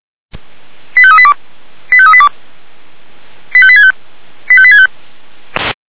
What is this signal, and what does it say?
Five-tone selective call (ZVEI), heard as quick runs of five tones, tones of about 70 ms. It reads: 85382 85382 85865 85865